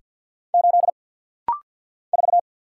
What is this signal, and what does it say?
Morse code, heard as a keyed tone, about 50 wpm, 700 Hz. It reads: 8 4
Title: Single Letter-Number - Rapid-Fire 50wpm